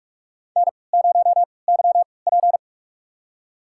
Morse code, heard as a keyed tone, about 46 wpm, spaced as for 25 wpm, 700 Hz. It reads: N0YP
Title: USA Call Signs - 4 Characters - 3x Character Spacing - 45wpm